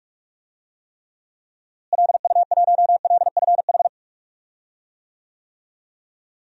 Morse code, heard as a keyed tone, about 45 wpm, 700 Hz. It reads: LU1LFH
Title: DXCC-100 Call Signs - 6 Characters - Repeated 3x - 45wpm